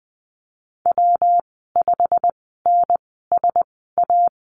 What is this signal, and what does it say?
Morse code, heard as a keyed tone, about 20 wpm, spaced as for 15 wpm, 700 Hz. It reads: W5NSA
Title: 161 Lesson 26g - Letter g - Call Signs 20wpm